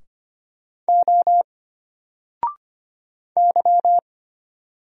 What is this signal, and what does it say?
Morse code, heard as a keyed tone, about 25 wpm, 700 Hz. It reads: O Y
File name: Single-Letter-Number-25wpm